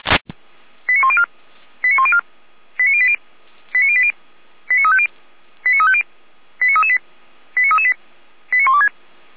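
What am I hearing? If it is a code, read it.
Five-tone selective call (ZVEI), heard as quick runs of five tones, tones of about 70 ms. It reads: 89104 89104 89080 89080 89350 89350 89338 89338 89126
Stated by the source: ZVEI five-tone call received on SDR for Feuerwehr Orlamünde.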